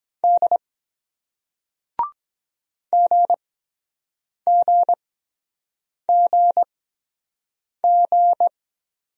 Morse code, heard as a keyed tone, about 24 wpm, spaced as for 7 wpm, 700 Hz. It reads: D GGGG